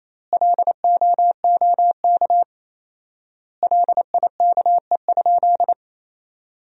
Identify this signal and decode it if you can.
Morse code, heard as a keyed tone, about 28 wpm, 700 Hz. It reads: LOOK LIKE?
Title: 4x Word-Spacing - Sentences from Top 100 Words - 28wpm